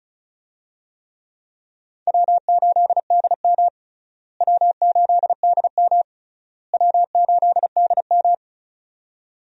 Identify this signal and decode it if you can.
Morse code, heard as a keyed tone, about 36 wpm, 700 Hz. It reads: W8DM W8DM W8DM
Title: USA Call Signs - 4 Characters - Repeated 3x - 35wpm